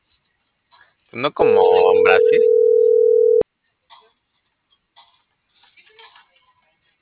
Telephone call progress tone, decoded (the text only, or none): ringback tone